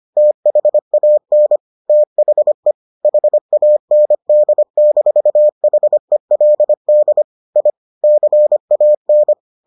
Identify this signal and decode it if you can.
Morse code, heard as a keyed tone, about 25 wpm, 600 Hz. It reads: THAN THE HAND-HELD I CAN